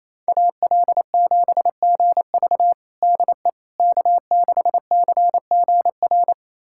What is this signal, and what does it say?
Morse code, heard as a keyed tone, about 27 wpm, 700 Hz. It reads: AL7GV DE K6CGR
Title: USA Callsign DE Callsign - 5 Characters - 28wpm